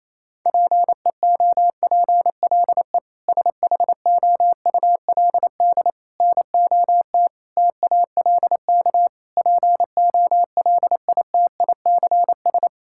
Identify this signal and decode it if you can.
Morse code, heard as a keyed tone, about 28 wpm, 700 Hz. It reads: PEOPLE SHOULD NOT TALK POLITICS
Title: Sentences from Top 1000 Words - Review Part 6 - 28wpm